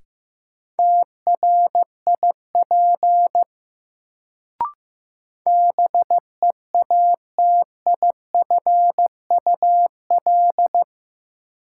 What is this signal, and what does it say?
Morse code, heard as a keyed tone, about 15 wpm, 700 Hz. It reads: TRIP BEATIFUL
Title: Sentences from Top 1300 Words - New Word Drill Alt - 15wpm